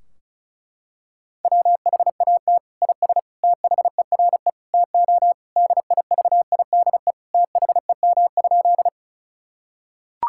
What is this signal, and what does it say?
Morse code, heard as a keyed tone, about 36 wpm, 700 Hz. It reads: WHAT IS THERE TO DIVIDE THEM?
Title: Sentences from Top 1600 Words 35wpm